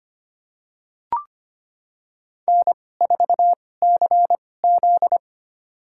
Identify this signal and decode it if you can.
Morse code, heard as a keyed tone, about 25 wpm, spaced as for 18 wpm, 700 Hz. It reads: N4CZ
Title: USA Call Signs - 4 Characters - 2x Character Spacing - 25wpm